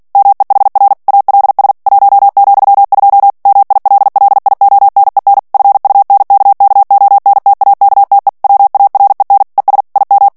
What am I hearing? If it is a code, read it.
Morse code, heard as a keyed tone, about 48 wpm, 800 Hz. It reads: MEHR ALS 1,2 MILLIONEN FUNKKONTAKTE WAREN ES IM
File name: morse mp3 / ALLEMAND texte clair 2 vitesse 45 mots minute